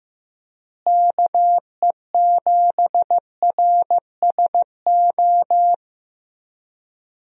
Morse code, heard as a keyed tone, about 16 wpm, 700 Hz. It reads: KE7RSO
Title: USA Call Signs - 6 Characters - Repeated 3x - 15wpm